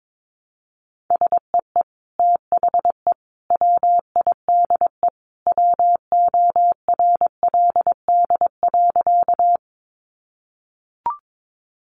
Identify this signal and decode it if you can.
Morse code, heard as a keyed tone, about 22 wpm, 700 Hz. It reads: SEE THE WIDE WORLD.
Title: Sentences from Top 900 Words 22wpm